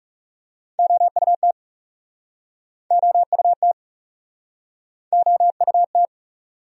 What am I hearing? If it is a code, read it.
Morse code, heard as a keyed tone, about 43 wpm, 700 Hz. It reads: OUT OUT OUT